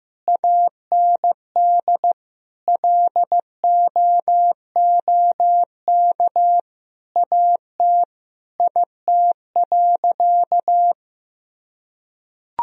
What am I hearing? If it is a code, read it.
Morse code, heard as a keyed tone, about 16 wpm, 700 Hz. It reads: AND LOOK AT IT.
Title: Sentences from Top 1000 Words - Review Part 2 - 15wpm